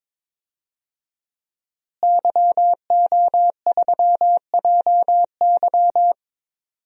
Morse code, heard as a keyed tone, about 21 wpm, 700 Hz. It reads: YO3JY
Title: DXCC-100 Call Signs - 5 Characters - Repeated 3x - 22wpm